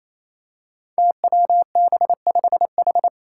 Morse code, heard as a keyed tone, about 28 wpm, 700 Hz. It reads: TWB5H
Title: Five Challenging Letter-Number Combinations 28wpm